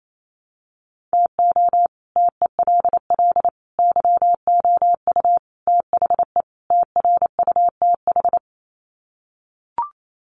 Morse code, heard as a keyed tone, about 28 wpm, 700 Hz. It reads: TO TELL YOU THE TRUTH
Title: Sets of 5 Words 28wpm